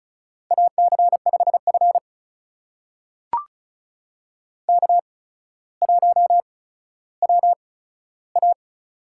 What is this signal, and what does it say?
Morse code, heard as a keyed tone, about 36 wpm, spaced as for 15 wpm, 700 Hz. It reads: AC5F K1WA
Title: USA Call Signs - 4 Characters - 8x Character Spacing - 35wpm